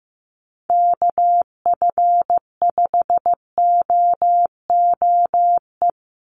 Morse code, heard as a keyed tone, about 15 wpm, 700 Hz. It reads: KF5OOE
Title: USA Call Signs 15wpm